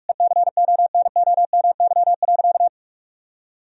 Morse code, heard as a keyed tone, about 46 wpm, 700 Hz. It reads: ECONOMY.
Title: Sentences from Top 1000 Words - Review Part 4 - 45wpm